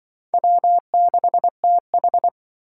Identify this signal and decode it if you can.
Morse code, heard as a keyed tone, about 24 wpm, 700 Hz. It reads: W6TH
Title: Speed Racing - USA 1x2 Call Signs 20wpm